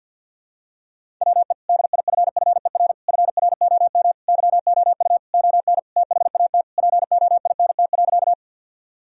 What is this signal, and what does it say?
Morse code, heard as a keyed tone, about 50 wpm, 700 Hz. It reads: WE DIFFER FROM YOU ON THAT POINT.
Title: Sentences from Top 1000 Words - Review Part 1 - 50wpm